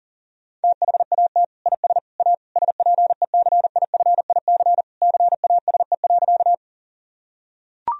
Morse code, heard as a keyed tone, about 40 wpm, 700 Hz. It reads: THAT IS A SPECIFIC CASE.
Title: Sentences from Top 1000 Words - Review Part 3 - 40wpm